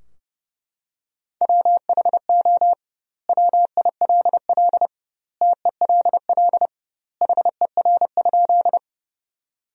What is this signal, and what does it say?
Morse code, heard as a keyed tone, about 30 wpm, 700 Hz. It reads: WHO WILL TELL HER?
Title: 2x Word-Spacing - Sentences from Top 200 Words - 30wpm